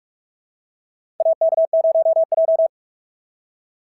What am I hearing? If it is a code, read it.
Morse code, heard as a keyed tone, about 46 wpm, 680 Hz. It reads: AK0J
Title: SOTA QSO Practice 45wpm